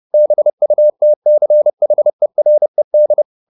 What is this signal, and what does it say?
Morse code, heard as a keyed tone, about 30 wpm, 600 Hz. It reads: BUTCHERED
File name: the old man and the sea / the old man and the sea 1 30wpm